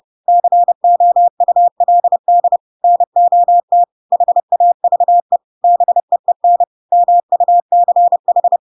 Morse code, heard as a keyed tone, about 30 wpm, 700 Hz. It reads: COULD NOT HAVE BEEN MUCH
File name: Pinocchio / Pinocchio ch3 30 30 ik1hge